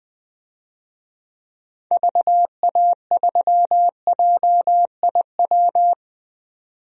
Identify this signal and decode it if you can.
Morse code, heard as a keyed tone, about 20 wpm, 700 Hz. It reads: VA3JIW
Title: DXCC-100 Call Signs - 6 Characters - 20wpm